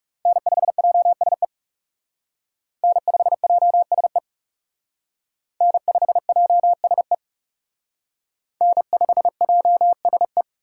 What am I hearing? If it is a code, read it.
Morse code, heard as a keyed tone, about 37 wpm, 700 Hz. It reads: N5JSE N5JSE N5JSE N5JSE
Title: Speed Racing - USA 1x3 Call Signs 30wpm